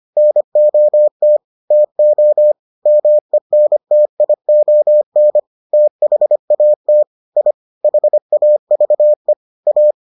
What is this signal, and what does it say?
Morse code, heard as a keyed tone, about 25 wpm, 600 Hz. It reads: NOT TO MENTION THAT I HAVE A